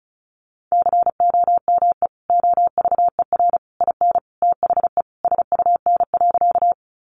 Morse code, heard as a keyed tone, about 34 wpm, 700 Hz. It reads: COME OVER IN THE SUN.
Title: Sentences from Top 300 Words 35wpm